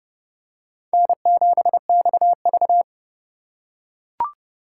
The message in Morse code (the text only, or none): N7XV